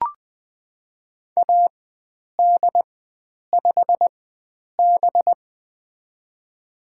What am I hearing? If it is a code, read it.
Morse code, heard as a keyed tone, about 21 wpm, spaced as for 9 wpm, 700 Hz. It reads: AD5B
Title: USA Call Signs - 4 Characters - 4x Character Spacing - 20wpm